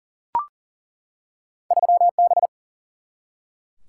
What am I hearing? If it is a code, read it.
Morse code, heard as a keyed tone, about 40 wpm, 700 Hz. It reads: 3B
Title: Single-Letter Single-Number 40wpm